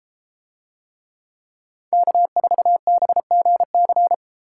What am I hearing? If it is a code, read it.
Morse code, heard as a keyed tone, about 32 wpm, 700 Hz. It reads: K4BGC